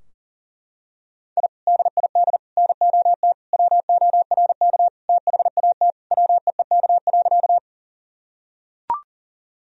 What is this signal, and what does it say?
Morse code, heard as a keyed tone, about 40 wpm, 700 Hz. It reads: I DID NOT WORK THAT WEEK.
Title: Sentences from Top 1000 Words - Review Part 2 - 40wpm